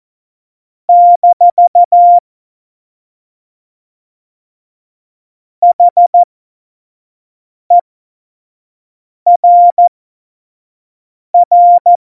Morse code, heard as a keyed tone, about 14 wpm, spaced as for 5 wpm, 700 Hz. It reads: - HERR
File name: Morsekurs / Witze / Witz 12. Dezember 2017, 25 F70